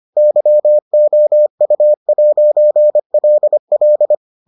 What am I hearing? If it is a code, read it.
Morse code, heard as a keyed tone, about 25 wpm, 600 Hz. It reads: YOU'LL